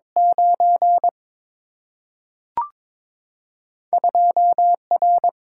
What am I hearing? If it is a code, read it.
Morse code, heard as a keyed tone, about 22 wpm, 700 Hz. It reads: 9 2R
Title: Single-Letter Single-Number 22wpm